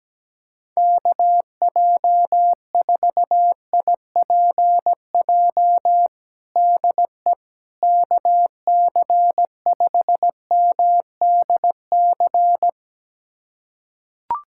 Morse code, heard as a keyed tone, about 17 wpm, 700 Hz. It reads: KJ4IPJ DE KC5MDC
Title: USA Callsign DE Callsign - 6 Characters - 17wpm